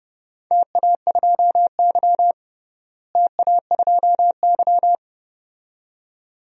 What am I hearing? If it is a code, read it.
Morse code, heard as a keyed tone, about 32 wpm, 700 Hz. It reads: TA2Y TA2Y
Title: DXCC-100 Call Signs - 4 Characters - Repeated 2x - 30wpm